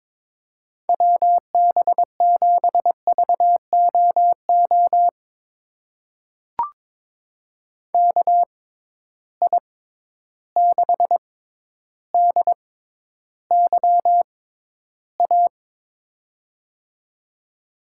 Morse code, heard as a keyed tone, about 22 wpm, spaced as for 11 wpm, 700 Hz. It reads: WB7VOO KI6DYA
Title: USA Call Signs - 6 Characters - 6x Character Spacing - 22wpm